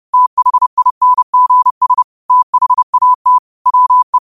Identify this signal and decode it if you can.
Morse code, heard as a keyed tone, about 30 wpm, 1000 Hz. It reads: THINGS THAT WE